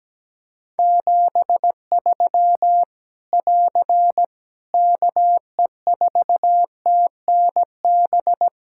Morse code, heard as a keyed tone, about 17 wpm, 700 Hz. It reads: 73 <AR> KE4TNB
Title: Practice QSO - Exchange 4 - Ending - 17wpm